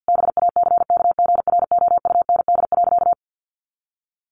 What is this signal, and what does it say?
Morse code, heard as a keyed tone, about 50 wpm, 700 Hz. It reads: BACKGROUND.
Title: Sentences from Top 1500 Words 50wpm